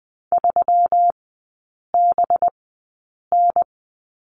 Morse code, heard as a keyed tone, about 20 wpm, spaced as for 8 wpm, 700 Hz. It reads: 3BN
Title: 184 Lesson 30e - Letter b - Three Characters 20wpm